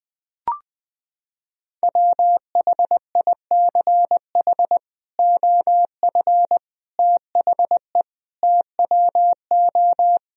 Morse code, heard as a keyed tone, about 20 wpm, 700 Hz. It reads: WHICH OF THE TWO
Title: Sentences from Top 1000 Words - Review Part 2 - 20wpm